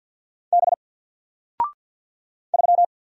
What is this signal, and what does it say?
Morse code, heard as a keyed tone, about 48 wpm, 700 Hz. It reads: B 3
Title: Single Letter-Number Alt 50wpm